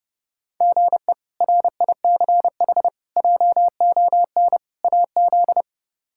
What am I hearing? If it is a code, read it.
Morse code, heard as a keyed tone, about 30 wpm, 700 Hz. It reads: GE RICH JON AZ
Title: K1USN SST QSO Practice - 30wpm